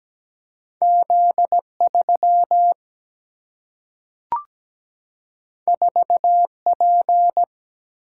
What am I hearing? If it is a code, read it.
Morse code, heard as a keyed tone, about 17 wpm, 700 Hz. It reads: Z 3 4 P
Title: Single-Letter Single-Number 17wpm